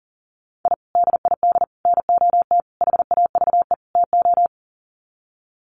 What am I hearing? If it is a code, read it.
Morse code, heard as a keyed tone, about 40 wpm, 700 Hz. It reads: I DID NOT HAVE TO
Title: Sets of 5 Words 40wpm